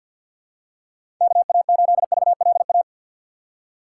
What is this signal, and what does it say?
Morse code, heard as a keyed tone, about 53 wpm, 700 Hz. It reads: KA8VLA
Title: USA Call Signs - 6x Character Spacing - 50wpm